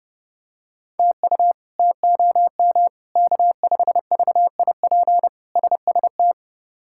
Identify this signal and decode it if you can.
Morse code, heard as a keyed tone, about 32 wpm, 700 Hz. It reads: TU TOM K5VIP SST
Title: K1USN SST QSO Practice - 30wpm